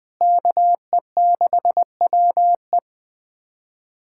Morse code, heard as a keyed tone, about 21 wpm, 700 Hz. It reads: KE6WE
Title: USA Callsign DE Callsign - 5 Characters - 20wpm